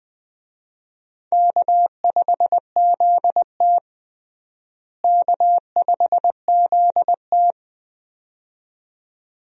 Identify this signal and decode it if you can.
Morse code, heard as a keyed tone, about 20 wpm, 700 Hz. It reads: K5ZT K5ZT
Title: USA Call Signs - 4 Characters - Repeated 2x - 20wpm